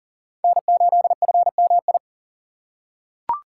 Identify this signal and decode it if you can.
Morse code, heard as a keyed tone, about 40 wpm, 700 Hz. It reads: N8FMI
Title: USA Call Signs - 5 Characters - 2x Character Spacing - 40wpm